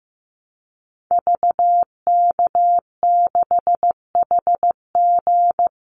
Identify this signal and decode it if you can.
Morse code, heard as a keyed tone, about 14 wpm, 700 Hz. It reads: VK6HG
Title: Call Signs Repeated 3x - 15wpm